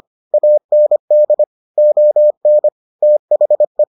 Morse code, heard as a keyed tone, about 25 wpm, 600 Hz. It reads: AND ON THE